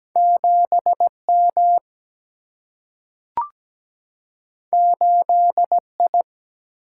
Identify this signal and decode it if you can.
Morse code, heard as a keyed tone, about 17 wpm, 700 Hz. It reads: 7 M 8 I
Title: Single-Letter Single-Number 17wpm